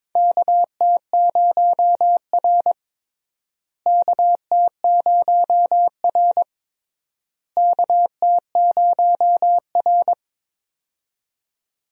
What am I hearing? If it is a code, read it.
Morse code, heard as a keyed tone, about 22 wpm, 700 Hz. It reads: KT0R KT0R KT0R
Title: USA Call Signs - 4 Characters - Repeated 3x - 22wpm